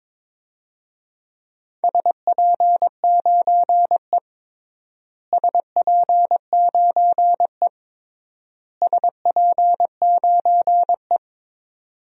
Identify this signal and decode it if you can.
Morse code, heard as a keyed tone, about 21 wpm, 700 Hz. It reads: SP9E SP9E SP9E
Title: DXCC-100 Call Signs - 4 Characters - Repeated 3x - 22wpm